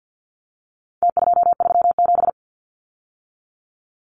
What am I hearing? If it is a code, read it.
Morse code, heard as a keyed tone, about 49 wpm, 700 Hz. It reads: T237
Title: Numbers - Contest Style - 4-Digit 50wpm